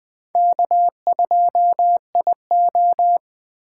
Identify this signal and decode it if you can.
Morse code, heard as a keyed tone, about 20 wpm, 700 Hz. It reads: K2IO
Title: Speed Racing - USA 1x2 Call Signs 17wpm